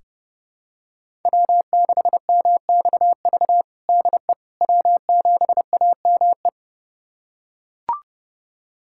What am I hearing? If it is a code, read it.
Morse code, heard as a keyed tone, about 32 wpm, 700 Hz. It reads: W6MXV DE W7AME
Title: USA Callsign DE Callsign - 5 Characters - 30wpm